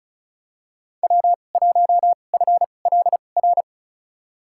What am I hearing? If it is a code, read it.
Morse code, heard as a keyed tone, about 34 wpm, spaced as for 25 wpm, 700 Hz. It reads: W1FLR
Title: USA Call Signs - 2x Character Spacing - 35wpm